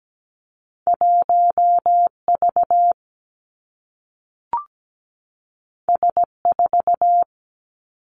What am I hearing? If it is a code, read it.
Morse code, heard as a keyed tone, about 17 wpm, 700 Hz. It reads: 1V S4
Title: Two Challenging Letter-Number Combinations 17wpm